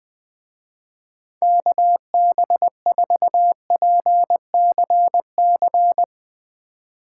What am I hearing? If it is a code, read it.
Morse code, heard as a keyed tone, about 21 wpm, 700 Hz. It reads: KB4PCC
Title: Speed Racing - USA Call Signs 17wpm